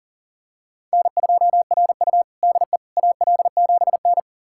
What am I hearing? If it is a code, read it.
Morse code, heard as a keyed tone, about 40 wpm, 700 Hz. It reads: N2RU DE AL7N
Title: USA Callsign DE Callsign - 4 Characters - 40wpm